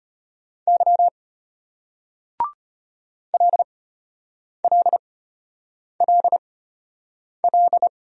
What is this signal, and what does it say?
Morse code, heard as a keyed tone, about 33 wpm, spaced as for 8 wpm, 700 Hz. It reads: Y LLLL